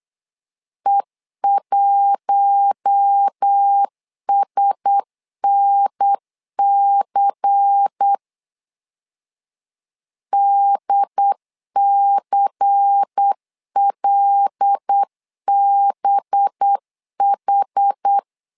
Morse code, heard as a keyed tone, about 8 wpm, 800 Hz. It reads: E1SNC DCLBH
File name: LEZIONI / 5 LBVC / 5 LBVC 8wpm 2935